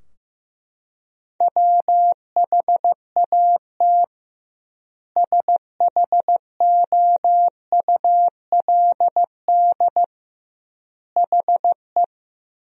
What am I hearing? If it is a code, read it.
Morse code, heard as a keyed tone, about 15 wpm, 700 Hz. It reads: WHAT SHOULD HE
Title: 2x Word-Spacing - Sentences from Top 300 Words - 15wpm